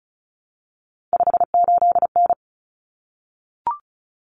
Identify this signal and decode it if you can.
Morse code, heard as a keyed tone, about 34 wpm, 700 Hz. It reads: 58N
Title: Common Cut Numbers - 3-Digit 35wpm